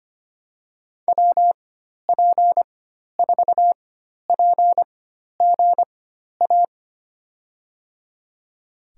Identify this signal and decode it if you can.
Morse code, heard as a keyed tone, about 25 wpm, spaced as for 12 wpm, 700 Hz. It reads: WP4PGA
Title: USA Call Signs - 6 Characters - 4x Character Spacing - 25wpm